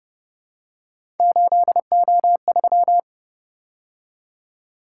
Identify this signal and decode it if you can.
Morse code, heard as a keyed tone, about 30 wpm, 700 Hz. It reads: 8O3